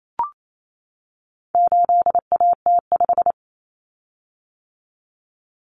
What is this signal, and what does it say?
Morse code, heard as a keyed tone, about 28 wpm, 700 Hz. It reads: 8AT5
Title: Common Cut Numbers - 4-Digit 28wpm